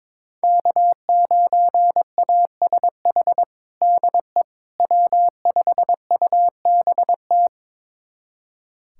Morse code, heard as a keyed tone, about 22 wpm, 700 Hz. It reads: K9ASH DE W5UBT
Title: USA Callsign DE Callsign - 5 Characters - 22wpm